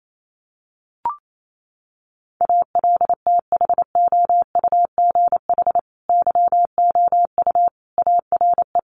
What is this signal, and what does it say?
Morse code, heard as a keyed tone, about 28 wpm, 700 Hz. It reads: ALTHOUGH YOU ARE
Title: Sentences from Top 1100 Words 28wpm